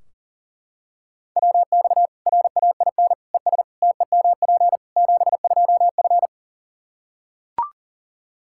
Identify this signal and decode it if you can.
Morse code, heard as a keyed tone, about 40 wpm, 700 Hz. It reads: WX RAIN ES TEMP 72F
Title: K1USN SST QSO Practice 40wpm